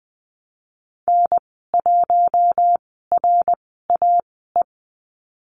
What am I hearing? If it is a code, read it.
Morse code, heard as a keyed tone, about 20 wpm, spaced as for 14 wpm, 700 Hz. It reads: N1RAE
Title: 069 Lesson 12g - Letter d - Call Signs 20wpm